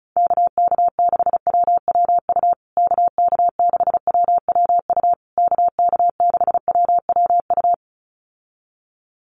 Morse code, heard as a keyed tone, about 34 wpm, 700 Hz. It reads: KK6WWU KK6WWU KK6WWU
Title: USA Call Signs Repeated 3x 35wpm